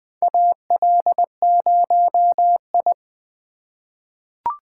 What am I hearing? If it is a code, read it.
Morse code, heard as a keyed tone, about 20 wpm, 700 Hz. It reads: AL0I
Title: USA Call Signs - 3x Character Spacing - 20wpm